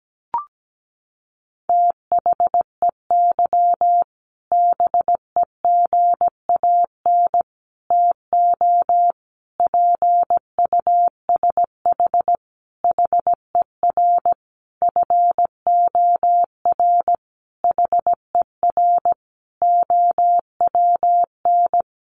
Morse code, heard as a keyed tone, about 17 wpm, 700 Hz. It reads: THEY BEGAN TO PUSH HER FOR HER OWN
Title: Sentences from Top 1300 Words 17wpm